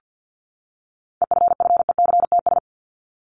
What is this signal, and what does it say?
Morse code, heard as a keyed tone, about 50 wpm, 700 Hz. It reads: EFFECTS
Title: ES:S Suffix - Top 100 Words with Base Words 50wpm